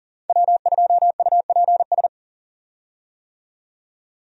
Morse code, heard as a keyed tone, about 37 wpm, 700 Hz. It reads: W2UPS
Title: USA Call Signs - 5 Characters - 40wpm